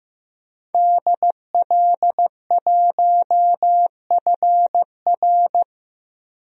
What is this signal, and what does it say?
Morse code, heard as a keyed tone, about 16 wpm, 700 Hz. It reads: DL1FR
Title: DXCC-100 Call Signs - 5 Characters - Repeated 2x - 15wpm